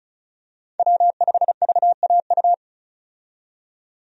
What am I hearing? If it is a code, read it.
Morse code, heard as a keyed tone, about 34 wpm, 700 Hz. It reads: W5VAU
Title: USA Call Signs - 5 Characters - 35wpm